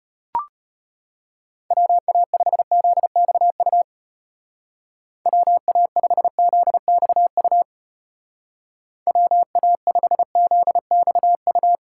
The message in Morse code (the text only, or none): WA5ZXU WA5ZXU WA5ZXU